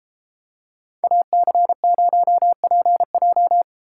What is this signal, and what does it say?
Morse code, heard as a keyed tone, about 30 wpm, 700 Hz. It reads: AC0PJ